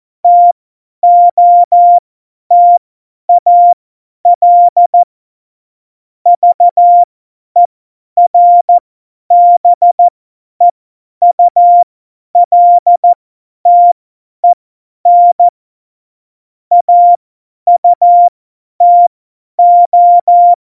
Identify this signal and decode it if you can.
Morse code, heard as a keyed tone, about 14 wpm, spaced as for 10 wpm, 700 Hz. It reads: TOTAL VERBEULTEN AUTO
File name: Morsekurs / Witze / Witz 12. Februar 2019, 50 F70